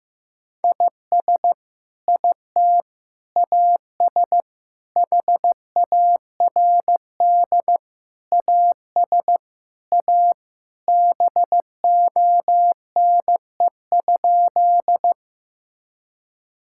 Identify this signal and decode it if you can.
Morse code, heard as a keyed tone, about 15 wpm, 700 Hz. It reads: IS IT AS HARD AS A BONE?
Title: Sentences from Top 1600 Words 15wpm